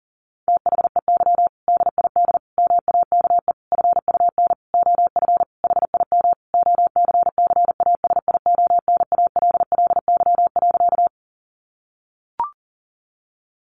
Morse code, heard as a keyed tone, about 40 wpm, 700 Hz. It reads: THEY DID MAKE FUN OF HIM OCCASIONALLY.
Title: Sentences from Top 1300 Words 40wpm